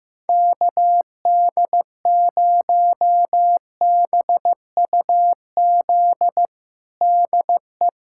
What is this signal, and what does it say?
Morse code, heard as a keyed tone, about 15 wpm, 700 Hz. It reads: KD0BUZ DE
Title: USA Callsign DE Callsign - 6 Characters - 15wpm